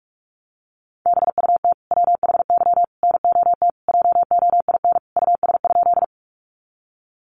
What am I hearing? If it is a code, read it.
Morse code, heard as a keyed tone, about 44 wpm, 700 Hz. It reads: BUT WHY NOT JOIN US?
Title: Sentences from Top 1400 Words 45wpm